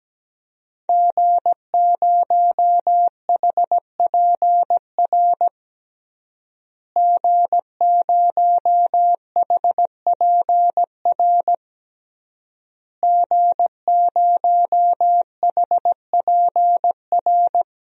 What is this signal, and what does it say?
Morse code, heard as a keyed tone, about 17 wpm, 700 Hz. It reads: G0HPR G0HPR G0HPR
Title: DXCC-100 Call Signs - 5 Characters - Repeated 3x - 17wpm